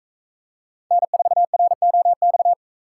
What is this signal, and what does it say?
Morse code, heard as a keyed tone, about 40 wpm, 700 Hz. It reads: N4ROX